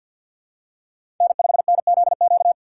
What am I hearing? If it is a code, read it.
Morse code, heard as a keyed tone, about 51 wpm, 700 Hz. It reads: N5NZQ
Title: USA Call Signs - 5 Characters - 4x Character Spacing - 50wpm